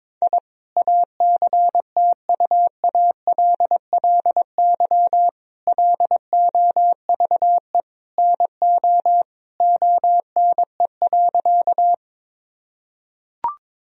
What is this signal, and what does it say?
Morse code, heard as a keyed tone, about 22 wpm, 700 Hz. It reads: I ACTUALLY LOVE NO ONE.
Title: Sentences from Top 1000 Words - Review Part 4 - 22wpm